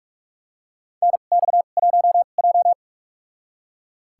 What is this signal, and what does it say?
Morse code, heard as a keyed tone, about 41 wpm, spaced as for 31 wpm, 700 Hz. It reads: NX1J
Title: USA Call Signs - 2x Character Spacing - 45wpm